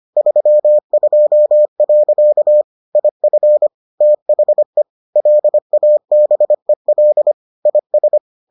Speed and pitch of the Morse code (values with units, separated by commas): 25 wpm, 600 Hz